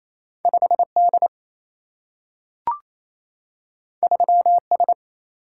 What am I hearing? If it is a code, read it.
Morse code, heard as a keyed tone, about 28 wpm, 700 Hz. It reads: 5 D 3 S
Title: Single-Letter Single-Number 28wpm